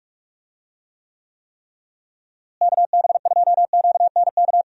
Morse code, heard as a keyed tone, about 46 wpm, 700 Hz. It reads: KB2QNK
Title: USA Call Signs - 6 Characters - Repeated 3x - 45wpm